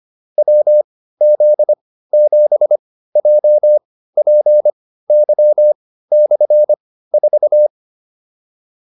Morse code, heard as a keyed tone, about 25 wpm, spaced as for 15 wpm, 600 Hz. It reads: WZ7JPY/4